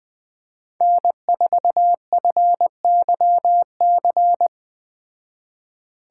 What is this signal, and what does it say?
Morse code, heard as a keyed tone, about 20 wpm, 700 Hz. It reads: N4FYC
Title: Speed Racing - USA 1x3 Call Signs 17wpm